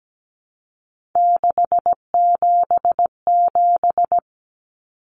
Morse code, heard as a keyed tone, about 17 wpm, 700 Hz. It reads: 677
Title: Numbers - 3-Digit 17wpm